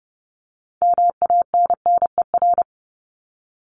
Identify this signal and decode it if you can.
Morse code, heard as a keyed tone, about 30 wpm, 700 Hz. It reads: MANNER
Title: Top 1501-1750 Words 30wpm